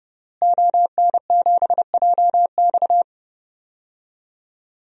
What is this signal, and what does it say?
Morse code, heard as a keyed tone, about 32 wpm, 700 Hz. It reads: ON7JX
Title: DXCC-100 Call Signs - 5 Characters - Repeated 3x - 30wpm